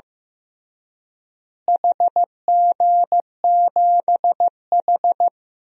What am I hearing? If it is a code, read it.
Morse code, heard as a keyed tone, about 16 wpm, 700 Hz. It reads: HG7H
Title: DXCC-100 Call Signs - 4 Characters - Repeated 3x - 15wpm